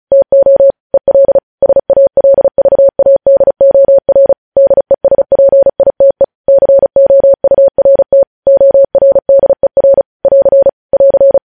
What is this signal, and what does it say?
Morse code, heard as a keyed tone, about 36 wpm, 550 Hz. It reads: TO EL SALVADOR DESPITE COURT ORDER <AR> <AR>
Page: News Headlines in Morse Code at 35 WPM